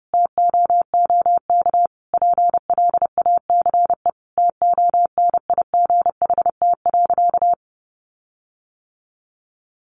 Morse code, heard as a keyed tone, about 30 wpm, 700 Hz. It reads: TOOK PLACE TONIGHT.
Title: Sentences from Top 1300 Words 30wpm